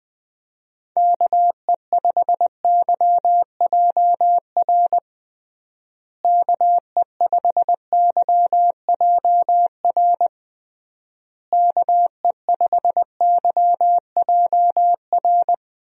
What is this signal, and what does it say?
Morse code, heard as a keyed tone, about 20 wpm, 700 Hz. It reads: KE5YJR KE5YJR KE5YJR
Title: USA Call Signs - 6 Characters - Repeated 3x - 20wpm